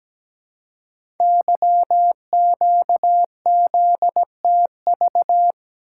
Morse code, heard as a keyed tone, about 17 wpm, 700 Hz. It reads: YQZTV
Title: Code Groups - 5 Letter:Numbers - 17wpm